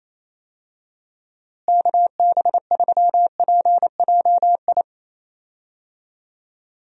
Morse code, heard as a keyed tone, about 28 wpm, 700 Hz. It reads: KB3PJI
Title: USA Call Signs - 6 Characters - Repeated 2x - 28wpm